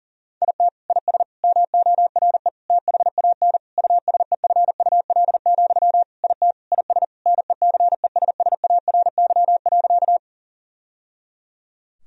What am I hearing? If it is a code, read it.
Morse code, heard as a keyed tone, about 40 wpm, 700 Hz. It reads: IT IS MORE THAN USEFUL, IT IS NECESSARY.
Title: Sentences from Top 1000 Words - Review Part 4 - 40wpm